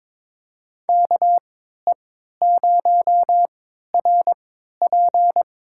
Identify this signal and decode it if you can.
Morse code, heard as a keyed tone, about 22 wpm, spaced as for 12 wpm, 700 Hz. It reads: KE0RP
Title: USA Call Signs - 3x Character Spacing - 22wpm